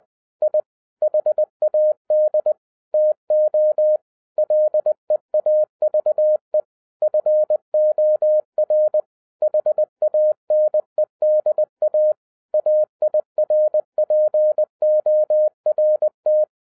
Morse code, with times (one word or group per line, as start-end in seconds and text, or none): 0.4-0.6 I
1.0-2.5 HAD
2.9-4.0 TO
4.4-6.6 LEAVE
7.0-9.0 FOR
9.4-12.1 HANEDA
12.5-16.4 AIRPORT